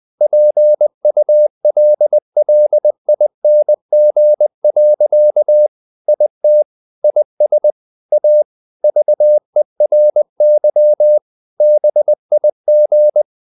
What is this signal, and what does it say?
Morse code, heard as a keyed tone, about 20 wpm, 600 Hz. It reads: PULLING. IT IS A VERY BIG
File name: the old man and the sea part 2 20wpm